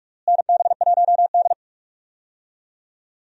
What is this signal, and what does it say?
Morse code, heard as a keyed tone, about 43 wpm, 700 Hz. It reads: NB1D
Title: USA Call Signs - 4 Characters - Repeated 3x - 45wpm